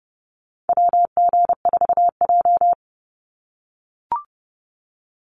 Morse code, heard as a keyed tone, about 29 wpm, 700 Hz. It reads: WG4J
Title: USA Call Signs 30wpm